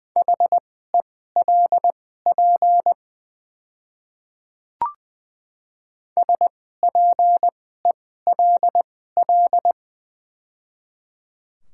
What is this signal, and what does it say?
Morse code, heard as a keyed tone, about 20 wpm, spaced as for 14 wpm, 700 Hz. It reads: HELP SPELL
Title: 2x Character Spacing - Top 200 Words - 20wpm